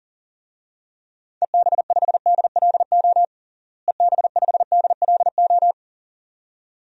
Morse code, heard as a keyed tone, about 40 wpm, 700 Hz. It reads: EB5DLO EB5DLO
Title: DXCC-100 Call Signs - 6 Characters - Repeated 2x - 40wpm